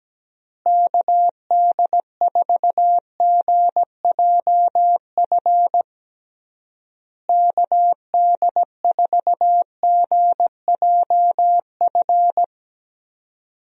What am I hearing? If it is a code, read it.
Morse code, heard as a keyed tone, about 17 wpm, 700 Hz. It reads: KD4GJF KD4GJF
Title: USA Call Signs - 6 Characters - Repeated 3x - 17wpm